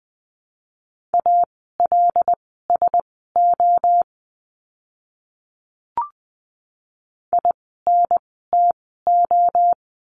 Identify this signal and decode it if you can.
Morse code, heard as a keyed tone, about 20 wpm, spaced as for 14 wpm, 700 Hz. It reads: ALSO INTO
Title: 244 Lesson 39f - Letter z - Words 20wpm